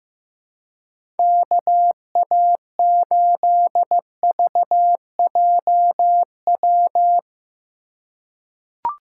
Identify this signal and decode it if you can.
Morse code, heard as a keyed tone, about 15 wpm, 700 Hz. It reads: KA8VJW
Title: USA Call Signs - 6 Characters - 2x Character Spacing - 15wpm